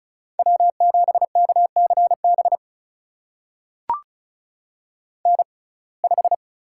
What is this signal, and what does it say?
Morse code, heard as a keyed tone, about 36 wpm, spaced as for 25 wpm, 700 Hz. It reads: W7KCB N5
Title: USA Call Signs - 5 Characters - 6x Character Spacing - 35wpm